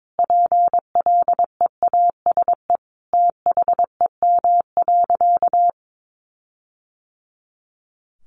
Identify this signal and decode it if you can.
Morse code, heard as a keyed tone, about 22 wpm, 700 Hz. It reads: PLEASE THEM.
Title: Sentences from Top 1500 Words 22wpm